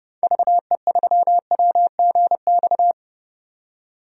Morse code, heard as a keyed tone, about 30 wpm, 700 Hz. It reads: VE3WGX
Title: DXCC-100 Call Signs - 6 Characters - 30wpm